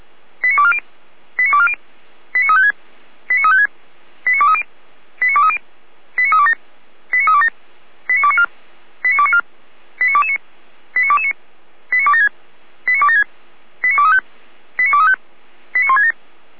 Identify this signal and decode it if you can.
Five-tone selective call (ZVEI), heard as quick runs of five tones, tones of about 70 ms. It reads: 89240 89240 89356 89356 89230 89230 89327 89327 89294 89294 89229 89229 89276 89276 89235 89235 89167